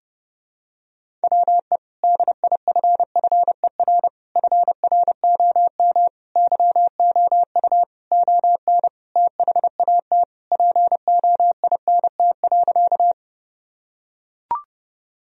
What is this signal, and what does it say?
Morse code, heard as a keyed tone, about 30 wpm, 700 Hz. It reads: WE DIFFER FROM YOU ON THAT POINT.
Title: Sentences from Top 1000 Words - Review Part 1 - 30wpm